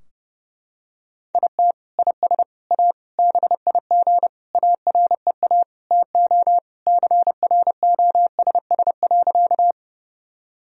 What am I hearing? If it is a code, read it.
Morse code, heard as a keyed tone, about 30 wpm, 700 Hz. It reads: IT IS A BIG AREA TO CROSS.
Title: Sentences from Top 1000 Words - Review Part 2 - 30wpm